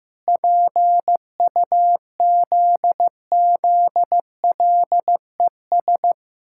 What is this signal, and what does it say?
Morse code, heard as a keyed tone, about 15 wpm, 700 Hz. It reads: PUZZLES
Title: Pangram Challenge v2 - 15wpm